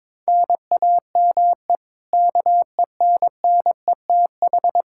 Morse code, heard as a keyed tone, about 22 wpm, 700 Hz. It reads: NAME KENNETH
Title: Practice QSO - Exchange 1 - Beginning - 22wpm